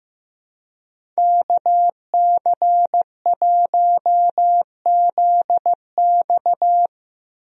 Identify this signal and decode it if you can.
Morse code, heard as a keyed tone, about 15 wpm, 700 Hz. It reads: KC1ZX
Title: USA Call Signs - 5 Characters - Repeated 2x - 15wpm